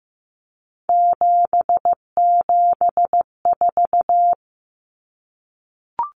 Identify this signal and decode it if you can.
Morse code, heard as a keyed tone, about 15 wpm, 700 Hz. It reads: 774
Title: Common Cut Numbers - 3-Digit 15wpm